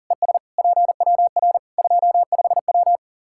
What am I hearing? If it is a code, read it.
Morse code, heard as a keyed tone, about 40 wpm, 700 Hz. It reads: ES PWR 25W
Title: Practice QSO - Exchange 2 - Info - 40wpm